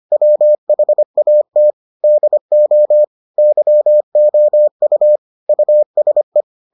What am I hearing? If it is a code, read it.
Morse code, heard as a keyed tone, about 25 wpm, 600 Hz. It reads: WHAT DO YOU USE